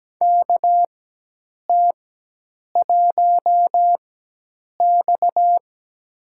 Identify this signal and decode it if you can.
Morse code, heard as a keyed tone, about 17 wpm, spaced as for 8 wpm, 700 Hz. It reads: KT1X
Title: USA Call Signs - 4x Character Spacing - 17wpm